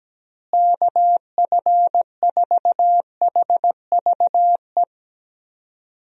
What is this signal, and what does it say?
Morse code, heard as a keyed tone, about 17 wpm, 700 Hz. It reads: KF4HVE
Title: USA Call Signs - 6 Characters - Repeated 2x - 17wpm